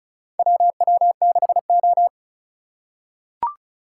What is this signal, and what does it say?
Morse code, heard as a keyed tone, about 34 wpm, 700 Hz. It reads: WW6O
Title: USA Call Signs - 4 Characters - 4x Character Spacing - 35wpm